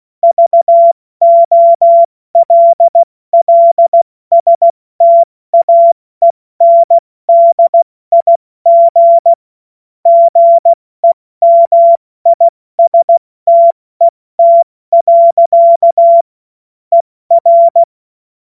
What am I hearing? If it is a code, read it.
Morse code, heard as a keyed tone, about 16 wpm, 680 Hz. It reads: VOLLSTAENDIG GEMISTET. ER
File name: Witz-16-September-2025-70-F80